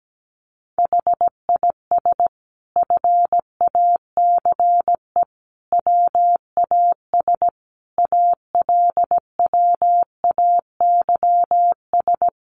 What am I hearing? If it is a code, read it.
Morse code, heard as a keyed tone, about 17 wpm, 700 Hz. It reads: HIS FACE WAS ALWAYS
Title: Sentences from Top 1200 Words 17wpm